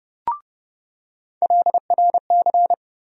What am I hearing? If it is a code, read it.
Morse code, heard as a keyed tone, about 30 wpm, 700 Hz. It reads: LRC
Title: USA Call Sign Postfixes - 3 Letters 30wpm